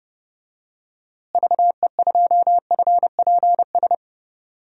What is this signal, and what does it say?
Morse code, heard as a keyed tone, about 30 wpm, 700 Hz. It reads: VE2FPS
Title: DXCC-100 Call Signs - 6 Characters - Repeated 3x - 30wpm